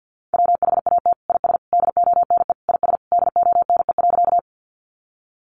Morse code, heard as a keyed tone, about 50 wpm, 700 Hz. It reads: WHAT IS DONE IS DONE.
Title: Sentences from Top 500 Words 50wpm